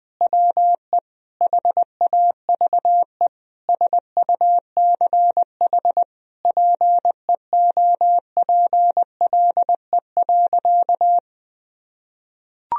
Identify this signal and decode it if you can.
Morse code, heard as a keyed tone, about 20 wpm, 700 Hz. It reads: WE HAVE SUCH PEOPLE.
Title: Sentences from Top 1000 Words - Review Part 3 - 20wpm